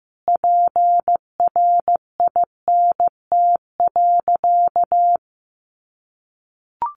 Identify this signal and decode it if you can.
Morse code, heard as a keyed tone, about 15 wpm, 700 Hz. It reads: PRINT.
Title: Sentences from Top 1400 Words 15wpm